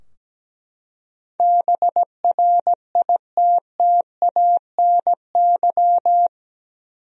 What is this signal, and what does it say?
Morse code, heard as a keyed tone, about 17 wpm, 700 Hz. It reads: BRITTANY
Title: USA Common Names - 17wpm